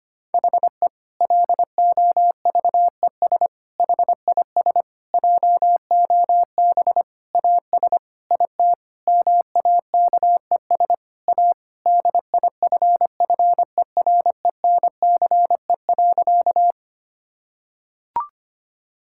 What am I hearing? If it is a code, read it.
Morse code, heard as a keyed tone, about 25 wpm, 700 Hz. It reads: HE LOVES HIS JOB AS IT MAKES A DIFFERENCE.
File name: practice / 25wpm / Difficult Sentences with Hints - 25wpm